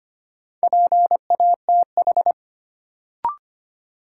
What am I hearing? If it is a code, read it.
Morse code, heard as a keyed tone, about 25 wpm, 700 Hz. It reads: PATH
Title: Sentences from Top 1400 Words - New Word Drill - 25wpm